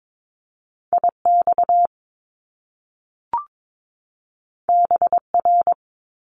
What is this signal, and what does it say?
Morse code, heard as a keyed tone, about 22 wpm, 700 Hz. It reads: IX BR
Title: Top Two Letter Combinations in Words 22wpm